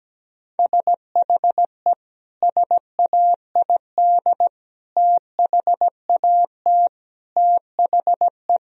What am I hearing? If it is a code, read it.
Morse code, heard as a keyed tone, about 17 wpm, 700 Hz. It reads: SHE SAID THAT THE
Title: Sentences from Top 1000 Words - Review Part 4 - 17wpm